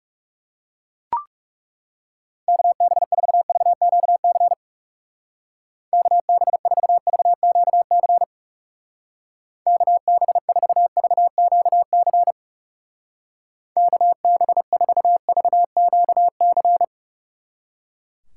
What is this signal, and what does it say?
Morse code, heard as a keyed tone, about 38 wpm, 700 Hz. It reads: KB4VQC KB4VQC KB4VQC KB4VQC